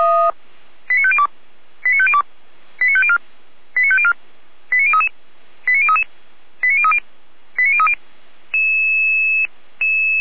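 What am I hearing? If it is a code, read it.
Five-tone selective call (ZVEI), heard as quick runs of five tones, tones of about 70 ms. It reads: 89502 89502 89504 89504 89033 89033 89030 89030
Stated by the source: ZVEI five-tone call 89001 received by SDR, feuerwehr-orlamuende fire brigade.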